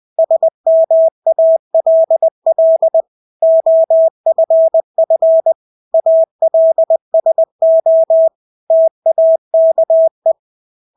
Morse code, heard as a keyed tone, about 20 wpm, 650 Hz. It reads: SMALL OFF ALSO TAKE
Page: Engelske ord 20 wpm